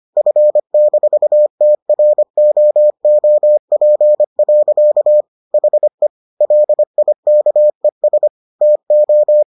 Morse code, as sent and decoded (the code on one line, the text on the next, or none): ..-. -....- - .-. --- --- .--. .-.-.- / .... / .-.. .. -.- . ... / - ---
F-TROOP. HE LIKES TO